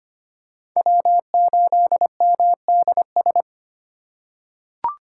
Morse code, heard as a keyed tone, about 25 wpm, 700 Hz. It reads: W8MDS
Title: USA Call Signs - 2x Character Spacing - 25wpm